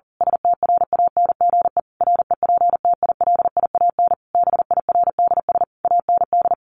Morse code, heard as a keyed tone, about 37 wpm, 700 Hz. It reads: STRANGE REPTILIAN BIRDS AND
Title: A Sound Of Thunder 40wpm